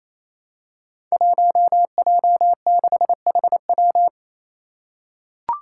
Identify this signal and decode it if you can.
Morse code, heard as a keyed tone, about 27 wpm, 700 Hz. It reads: 1J6HW
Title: Five Challenging Letter-Number Combinations 28wpm